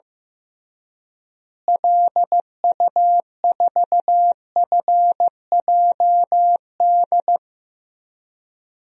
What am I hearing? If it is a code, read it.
Morse code, heard as a keyed tone, about 16 wpm, 700 Hz. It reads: LU4FJD
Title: DXCC-100 Call Signs - 6 Characters - Repeated 3x - 15wpm